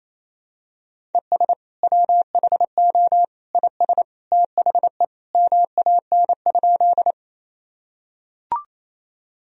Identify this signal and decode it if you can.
Morse code, heard as a keyed tone, about 28 wpm, 700 Hz. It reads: ES WHO IS THE MAN?
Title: Sentences from Top 200 Words and Abbreviations 28wpm